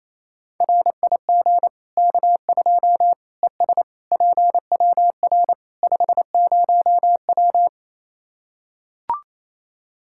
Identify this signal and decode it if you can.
Morse code, heard as a keyed tone, about 28 wpm, 700 Hz. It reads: RIG K2 ES PWR 50W
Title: Practice QSO - Exchange 2 - Info - 28wpm